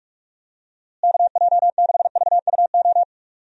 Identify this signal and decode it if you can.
Morse code, heard as a keyed tone, about 46 wpm, 700 Hz. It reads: KJ6VUO
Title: USA Call Signs - 6 Characters - Repeated 3x - 45wpm